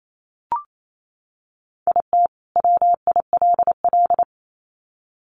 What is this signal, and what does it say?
Morse code, heard as a keyed tone, about 28 wpm, 700 Hz. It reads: IT WILL
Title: Sets of 2 Words 28wpm